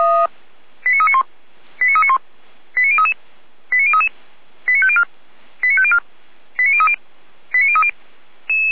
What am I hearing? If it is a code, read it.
Five-tone selective call (ZVEI), heard as quick runs of five tones, tones of about 80 ms. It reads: 89391 89391 89033 89033 89504 89504 89030 89030
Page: SDR/ZVEI